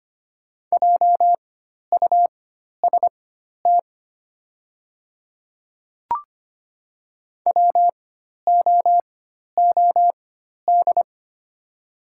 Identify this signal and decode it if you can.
Morse code, heard as a keyed tone, about 25 wpm, spaced as for 12 wpm, 700 Hz. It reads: JUST WOOD
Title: 4x Character Spacing - Top 300 Words - 25wpm